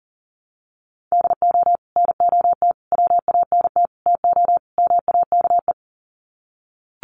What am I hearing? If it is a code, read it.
Morse code, heard as a keyed tone, about 40 wpm, 700 Hz. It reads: DO NOT WANT TO MAKE
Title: Sets of 5 Words Encore 40wpm